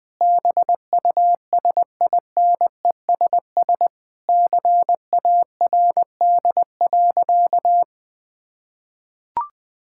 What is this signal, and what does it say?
Morse code, heard as a keyed tone, about 20 wpm, 700 Hz. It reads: BUSINESS CARD.
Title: Sentences from Top 1000 Words - Review Part 6 - 20wpm